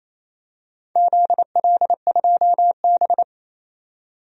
Morse code, heard as a keyed tone, about 28 wpm, 700 Hz. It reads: ZL2B
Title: DXCC-100 Call Signs - 4 Characters - 28wpm